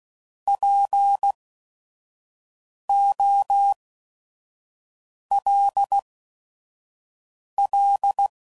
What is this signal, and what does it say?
Morse code, heard as a keyed tone, about 16 wpm, spaced as for 5 wpm, 800 Hz. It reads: POLL
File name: Prefixes / By First Letter / Ps